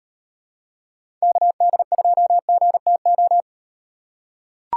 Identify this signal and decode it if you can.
Morse code, heard as a keyed tone, about 38 wpm, 700 Hz. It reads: KD2GTO